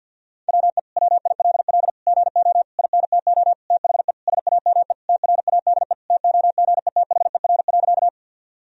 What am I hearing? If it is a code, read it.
Morse code, heard as a keyed tone, about 50 wpm, 700 Hz. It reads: WE WILL GO INTO THE SAME TRADE TOGETHER.
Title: Sentences from Top 1000 Words - Review Part 4 - 50wpm